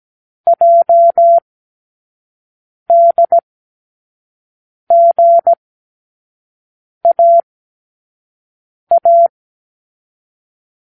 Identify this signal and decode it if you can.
Morse code, heard as a keyed tone, about 17 wpm, 680 Hz. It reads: J D G A A
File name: k7qo / Lesson15(A-J)